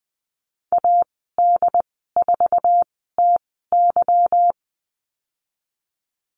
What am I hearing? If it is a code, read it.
Morse code, heard as a keyed tone, about 20 wpm, spaced as for 14 wpm, 700 Hz. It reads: AD4TY
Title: 147 Lesson 24g - Letter y - Call Signs 20wpm